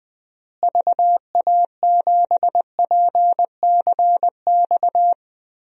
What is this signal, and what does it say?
Morse code, heard as a keyed tone, about 21 wpm, 700 Hz. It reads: VA7PCX
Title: DXCC-100 Call Signs - 6 Characters - Repeated 3x - 20wpm